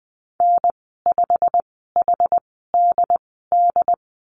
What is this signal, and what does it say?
Morse code, heard as a keyed tone, about 20 wpm, spaced as for 15 wpm, 700 Hz. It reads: N5HDD
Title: 186 Lesson 30g - Letter b - Call Signs 20wpm